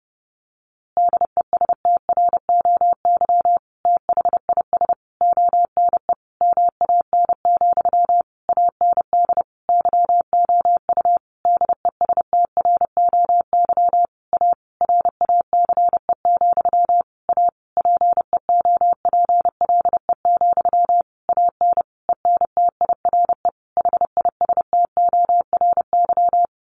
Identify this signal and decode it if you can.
Morse code, heard as a keyed tone, about 30 wpm, 700 Hz. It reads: DESTROY THIS ONE MAN, AND YOU DESTROY A RACE, A PEOPLE, AN ENTIRE HISTORY
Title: A Sound Of Thunder 30wpm